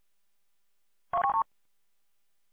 There is no code.